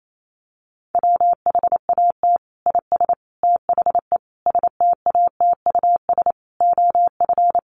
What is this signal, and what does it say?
Morse code, heard as a keyed tone, about 28 wpm, 700 Hz. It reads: WHAT IS THE STATUS OF
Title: Sentences from Top 1400 Words 28wpm